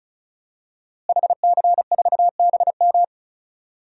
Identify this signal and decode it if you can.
Morse code, heard as a keyed tone, about 36 wpm, 700 Hz. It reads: HC4BM
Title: DXCC-100 Call Signs - 5 Characters - Repeated 3x - 35wpm